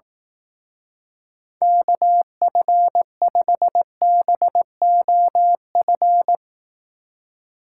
Morse code, heard as a keyed tone, about 18 wpm, 700 Hz. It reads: KF5BOF
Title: Speed Racing - USA 2x3 Call Signs 15wpm